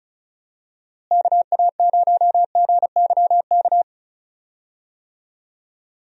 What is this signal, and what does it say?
Morse code, heard as a keyed tone, about 36 wpm, 700 Hz. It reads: KA0GYK
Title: USA Call Signs - 6 Characters - Repeated 2x - 35wpm